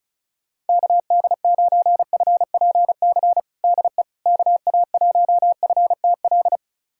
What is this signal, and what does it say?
Morse code, heard as a keyed tone, about 35 wpm, 700 Hz. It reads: KD9FPC DE KA1FTL
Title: USA Callsign DE Callsign - 6 Characters - 35wpm